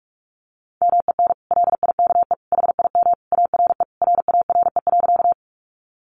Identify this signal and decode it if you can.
Morse code, heard as a keyed tone, about 46 wpm, 700 Hz. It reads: MEN LIKE HIM ARE RARE.
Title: Sentences from Top 1100 Words 45wpm